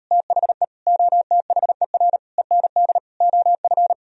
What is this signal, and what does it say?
Morse code, heard as a keyed tone, about 38 wpm, 700 Hz. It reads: THE OTHER END OF